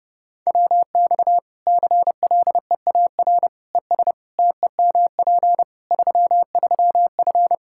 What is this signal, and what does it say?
Morse code, heard as a keyed tone, about 30 wpm, 700 Hz. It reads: WX CLEAR ES TEMP 33F
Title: Practice QSO - Exchange 2 - Info - 30wpm